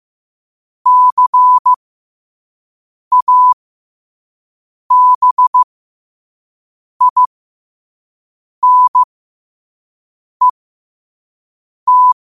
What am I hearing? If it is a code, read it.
Morse code, heard as a keyed tone, about 15 wpm, spaced as for 5 wpm, 1000 Hz. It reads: CABINET